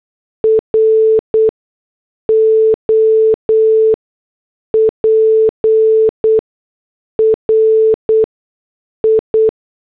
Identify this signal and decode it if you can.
Morse code, heard as a keyed tone, about 8 wpm, spaced as for 6 wpm, 440 Hz. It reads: ROPRI
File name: morse / headlines